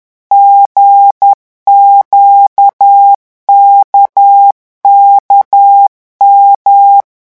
Morse code, heard as a keyed tone, about 10 wpm, 800 Hz. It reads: GQKKM
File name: morse mp3 / revision3-vitesse 10 mots